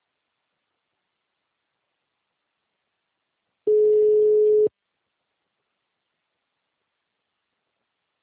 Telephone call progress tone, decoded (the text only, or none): ringback tone